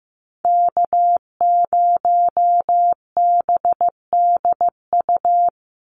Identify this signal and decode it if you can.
Morse code, heard as a keyed tone, about 15 wpm, 700 Hz. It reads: K0BDU
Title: USA Call Signs 15wpm